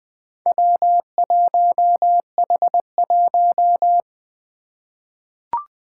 Morse code, heard as a keyed tone, about 20 wpm, 700 Hz. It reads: W1H1
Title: Four Challenging Letter-Number Combinations 20wpm